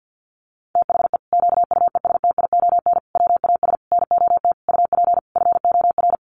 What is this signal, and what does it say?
Morse code, heard as a keyed tone, about 50 wpm, 700 Hz. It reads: THE QUESTION WAS NOT UP FOR
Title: Sentences from Top 1200 Words 50wpm